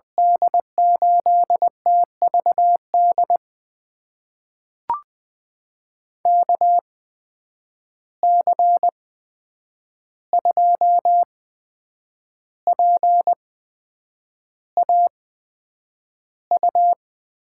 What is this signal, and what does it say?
Morse code, heard as a keyed tone, about 20 wpm, spaced as for 8 wpm, 700 Hz. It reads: D8TVD KC2PAU
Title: USA Call Signs - 6 Characters - 8x Character Spacing - 20wpm